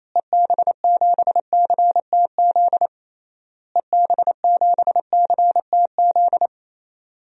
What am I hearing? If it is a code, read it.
Morse code, heard as a keyed tone, about 28 wpm, 700 Hz. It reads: EB7CTZ EB7CTZ
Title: DXCC-100 Call Signs - 6 Characters - Repeated 3x - 28wpm